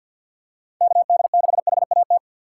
Morse code, heard as a keyed tone, about 47 wpm, 700 Hz. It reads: KD6HAT
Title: USA Call Signs - 3x Character Spacing - 50wpm